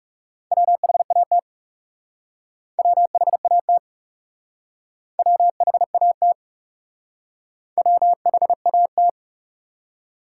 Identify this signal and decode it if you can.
Morse code, heard as a keyed tone, about 38 wpm, 700 Hz. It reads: WHAT WHAT WHAT WHAT